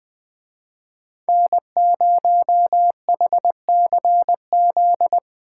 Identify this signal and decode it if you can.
Morse code, heard as a keyed tone, about 20 wpm, 700 Hz. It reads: N0HCZ
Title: Speed Racing - USA 1x3 Call Signs 17wpm